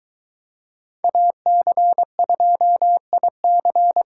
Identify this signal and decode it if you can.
Morse code, heard as a keyed tone, about 23 wpm, 700 Hz. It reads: AC2IC